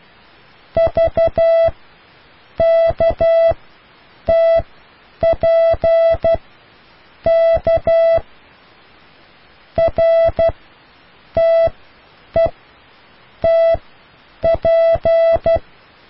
Morse code, heard as a keyed tone, about 12 wpm, spaced as for 7 wpm, 660 Hz. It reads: VKTPK RTETP